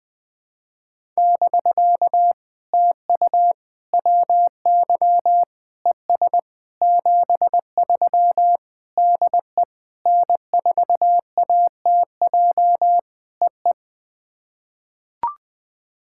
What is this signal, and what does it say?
Morse code, heard as a keyed tone, about 20 wpm, 700 Hz. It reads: <BK> TU WY ES 73 DE N4ATJ EE
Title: POTA QSO Practice - 20wpm